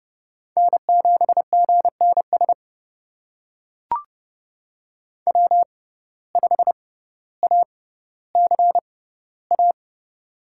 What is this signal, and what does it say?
Morse code, heard as a keyed tone, about 30 wpm, spaced as for 15 wpm, 700 Hz. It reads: N7GNS W5ACA
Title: USA Call Signs - 6x Character Spacing - 30wpm